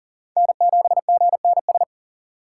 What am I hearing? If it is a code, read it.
Morse code, heard as a keyed tone, about 40 wpm, 700 Hz. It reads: N7GNS
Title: USA Call Signs - 5 Characters - 2x Character Spacing - 40wpm